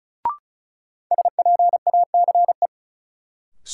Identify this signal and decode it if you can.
Morse code, heard as a keyed tone, about 36 wpm, 700 Hz. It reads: SPACE
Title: Sentences from Top 1000 Words - Review Part 5 - New Word Drill - 35wpm